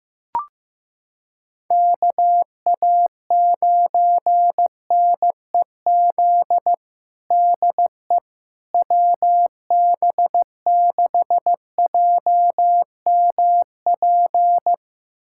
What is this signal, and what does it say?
Morse code, heard as a keyed tone, about 16 wpm, 700 Hz. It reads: KA9NEZ DE WB6JMP
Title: USA Callsign DE Callsign 15wpm